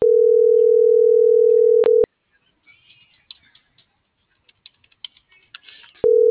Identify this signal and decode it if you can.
Ringback tone (North America), 440 + 480 Hz.